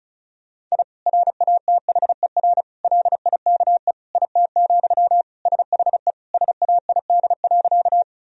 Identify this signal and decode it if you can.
Morse code, heard as a keyed tone, about 34 wpm, 700 Hz. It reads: I RATHER LIKE IT, SHE SAID.
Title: Sentences from Top 1000 Words - Review Part 4 - 35wpm